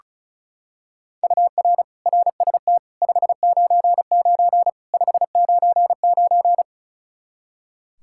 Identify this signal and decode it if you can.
Morse code, heard as a keyed tone, about 34 wpm, 700 Hz. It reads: UR RST 599 599
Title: K1USN SST QSO Practice 35wpm